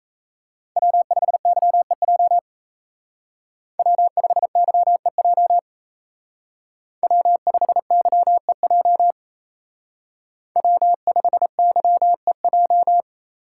Morse code, heard as a keyed tone, about 33 wpm, 700 Hz. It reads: W5YEJ W5YEJ W5YEJ W5YEJ
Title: Speed Racing - USA 1x3 Call Signs 28wpm